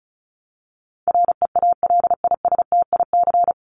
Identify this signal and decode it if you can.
Morse code, heard as a keyed tone, about 34 wpm, 700 Hz. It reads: REALISTIC
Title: Top 1751-2000 Words 35wpm